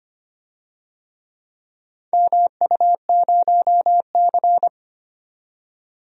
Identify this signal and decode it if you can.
Morse code, heard as a keyed tone, about 25 wpm, 700 Hz. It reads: MU0C
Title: DXCC-100 Call Signs - 4 Characters - 25wpm